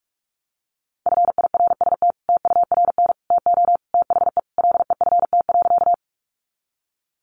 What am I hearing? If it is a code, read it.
Morse code, heard as a keyed tone, about 44 wpm, 700 Hz. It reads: FIRST TURN TO THE LEFT.
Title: Sentences from Top 300 Words 45wpm